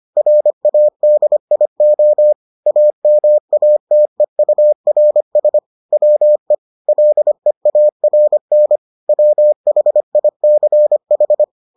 Morse code, heard as a keyed tone, about 25 wpm, 600 Hz. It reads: RADIO AMATEURS WE LEARN WHICH